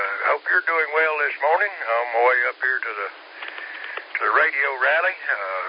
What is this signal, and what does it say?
Emergency Alert System (SAME) tones (header bursts), heard as harsh buzzing bursts, with speech. It is heard from 0.4 to 5.1 s.